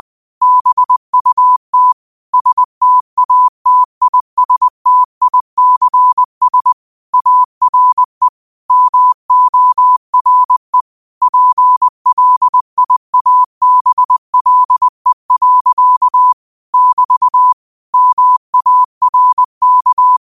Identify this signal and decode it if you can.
Morse code, heard as a keyed tone, about 20 wpm, 1000 Hz. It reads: BUT STATISTICS ARE MORE PLIABLE. <BT> MARK
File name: qotdfiles / 20wpm